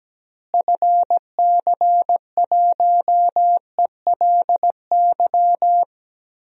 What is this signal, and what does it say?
Morse code, heard as a keyed tone, about 17 wpm, 700 Hz. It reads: FC1ELY
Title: DXCC-100 Call Signs - 6 Characters - Repeated 2x - 17wpm